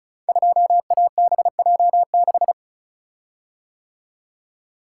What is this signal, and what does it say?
Morse code, heard as a keyed tone, about 36 wpm, 700 Hz. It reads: 2ABJ6
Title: Code Groups - 5 Letter:Numbers - 35wpm